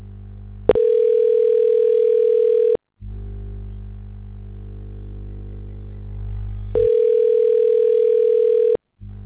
Ringback tone (North America), 440 + 480 Hz.